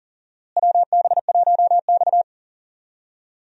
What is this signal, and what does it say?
Morse code, heard as a keyed tone, about 40 wpm, 700 Hz. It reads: WB1X
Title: USA Call Signs - 4 Characters - Repeated 2x - 40wpm